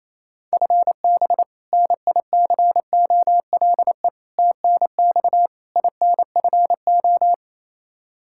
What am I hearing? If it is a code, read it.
Morse code, heard as a keyed tone, about 28 wpm, 700 Hz. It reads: FB NICOLE TNX INFO
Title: Practice QSO - Exchange 3 - More Info - 28wpm